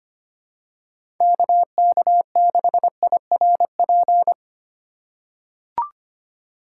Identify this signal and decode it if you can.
Morse code, heard as a keyed tone, about 25 wpm, 700 Hz. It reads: KK6IRP
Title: USA Call Signs - 3x Character Spacing - 25wpm